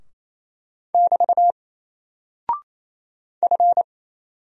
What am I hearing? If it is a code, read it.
Morse code, heard as a keyed tone, about 27 wpm, 700 Hz. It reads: <BT> F
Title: Single Character (Letter:Number:Punctuation:Prosign) Alt 28wpm